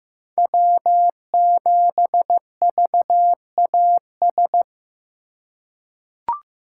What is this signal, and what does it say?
Morse code, heard as a keyed tone, about 15 wpm, 700 Hz. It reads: W7VAS
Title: K1USN SST QSO Practice - 15wpm